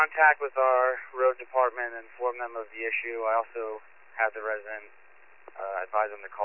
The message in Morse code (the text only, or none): none